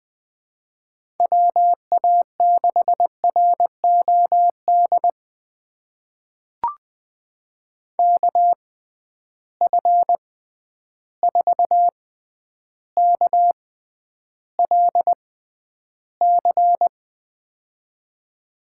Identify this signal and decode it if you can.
Morse code, heard as a keyed tone, about 20 wpm, spaced as for 10 wpm, 700 Hz. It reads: WA6ROD KF4KLC
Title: USA Call Signs - 6 Characters - 6x Character Spacing - 20wpm